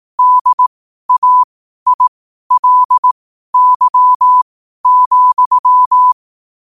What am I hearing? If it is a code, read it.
Morse code, heard as a keyed tone, about 18 wpm, spaced as for 12 wpm, 1000 Hz. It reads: DAILY,
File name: qotdfiles / 12wpm